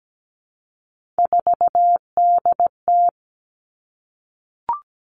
Morse code, heard as a keyed tone, about 17 wpm, 700 Hz. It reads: 4DT
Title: All Cut Numbers - 3-Digit 17wpm